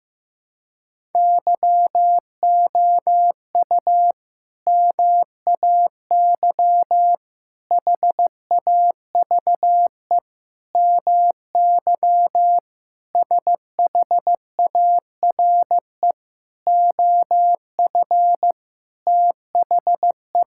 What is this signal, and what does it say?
Morse code, heard as a keyed tone, about 15 wpm, 700 Hz. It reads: YOU MAY HAVE MY SHARE OF THE
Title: Sentences from Top 1000 Words - Review Part 6 - 15wpm